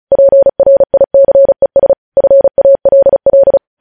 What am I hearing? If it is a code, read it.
Morse code, heard as a keyed tone, about 36 wpm, 550 Hz. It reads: PRICES FALL
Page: news headlines in morse code at 35 wpm